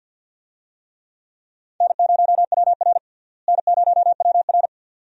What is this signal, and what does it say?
Morse code, heard as a keyed tone, about 51 wpm, 700 Hz. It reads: N0WR N0WR
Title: USA Call Signs - 4 Characters - Repeated 2x - 50wpm